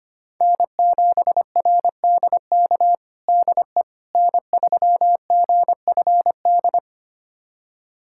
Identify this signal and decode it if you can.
Morse code, heard as a keyed tone, about 25 wpm, 700 Hz. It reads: N7RDK DE N3GFD
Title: USA Callsign DE Callsign - 5 Characters - 25wpm